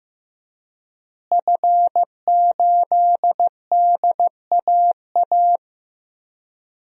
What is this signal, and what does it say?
Morse code, heard as a keyed tone, about 15 wpm, 700 Hz. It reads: F8DAA
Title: DXCC-100 Call Signs - 5 Characters - Repeated 3x - 15wpm